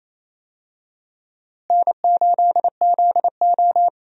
Morse code, heard as a keyed tone, about 28 wpm, 700 Hz. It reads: N8ZO
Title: USA Call Signs - 4 Characters - Repeated 2x - 28wpm